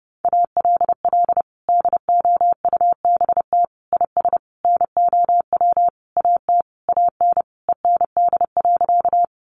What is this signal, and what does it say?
Morse code, heard as a keyed tone, about 30 wpm, 700 Hz. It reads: ALL DOUBT IS NOW AT AN END.
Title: Sentences from Top 1500 Words 30wpm